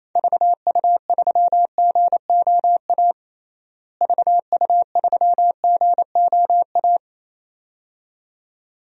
Morse code, heard as a keyed tone, about 28 wpm, 700 Hz. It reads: VU3GOA VU3GOA
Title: DXCC-100 Call Signs - 6 Characters - Repeated 2x - 28wpm